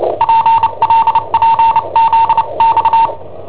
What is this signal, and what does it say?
Morse code, heard as a keyed tone, about 29 wpm, 910 Hz. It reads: PLPZX
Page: Cassetta esercizi preparata da I7OHP per IW5ELP